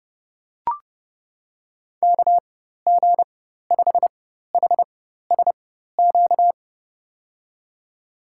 Morse code, heard as a keyed tone, about 30 wpm, spaced as for 14 wpm, 700 Hz. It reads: KG5HSQ
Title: USA Call Signs - 6 Characters - 4x Character Spacing - 30wpm